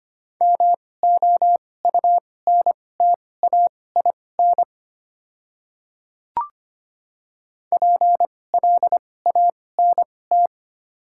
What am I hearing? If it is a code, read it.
Morse code, heard as a keyed tone, about 25 wpm, spaced as for 18 wpm, 700 Hz. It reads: MOUNTAIN PLANT
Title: 2x Character Spacing - Top 300 Words - 25wpm